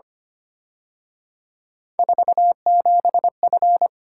Morse code, heard as a keyed tone, about 29 wpm, 700 Hz. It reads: 47F
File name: Speed-Racing - Three Letter-Number Combinations - 25wpm